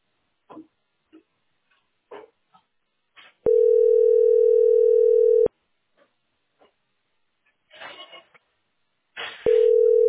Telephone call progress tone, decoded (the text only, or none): ringback tone